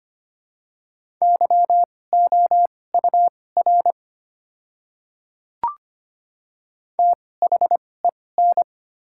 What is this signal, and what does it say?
Morse code, heard as a keyed tone, about 25 wpm, spaced as for 18 wpm, 700 Hz. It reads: YOUR THEN
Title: 2x Character Spacing - Top 100 Words - 25wpm